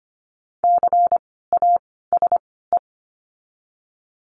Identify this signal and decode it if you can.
Morse code, heard as a keyed tone, about 25 wpm, spaced as for 16 wpm, 700 Hz. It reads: CASE
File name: course / 25:10wpm / 094 Lesson 16f - Letter c - Words 25wpm